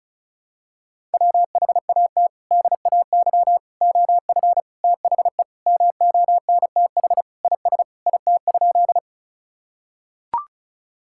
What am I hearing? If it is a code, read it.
Morse code, heard as a keyed tone, about 36 wpm, 700 Hz. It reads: WHAT DAY OF THE MONTH IS IT?
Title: Sentences from Top 1000 Words - Review Part 5 - 35wpm